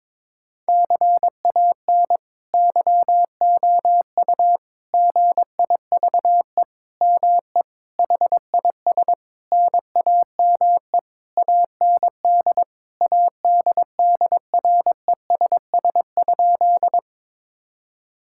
Morse code, heard as a keyed tone, about 22 wpm, 700 Hz. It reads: CAN YOU GIVE ME HIS NAME AND ADDRESS?
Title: Sentences from Top 1000 Words - Review Part 6 - 22wpm